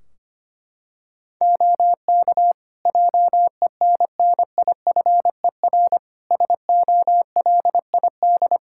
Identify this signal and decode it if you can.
Morse code, heard as a keyed tone, about 25 wpm, 700 Hz. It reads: OK JENNIFER SOLID
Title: Practice QSO - Exchange 3 - More Info - 25wpm